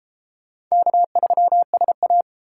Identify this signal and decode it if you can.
Morse code, heard as a keyed tone, about 32 wpm, 700 Hz. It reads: K3SA